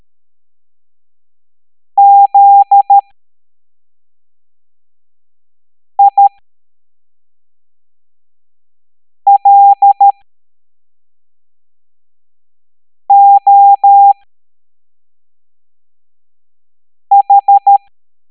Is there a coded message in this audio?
Yes: Morse code from 2.0 to 17.8 s, heard as a keyed tone, about 13 wpm, spaced as for 3 wpm, 800 Hz. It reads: ZILOH